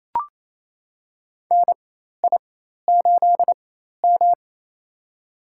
Morse code, heard as a keyed tone, about 27 wpm, spaced as for 13 wpm, 700 Hz. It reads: NI8M
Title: USA Call Signs - 4 Characters - 4x Character Spacing - 28wpm